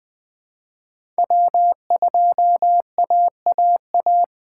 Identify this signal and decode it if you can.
Morse code, heard as a keyed tone, about 20 wpm, 700 Hz. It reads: W2AAA
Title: USA Call Signs - 5 Characters - 2x Character Spacing - 20wpm